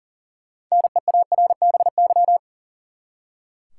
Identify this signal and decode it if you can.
Morse code, heard as a keyed tone, about 40 wpm, 700 Hz. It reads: NEARBY
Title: Six Letter Words - 40wpm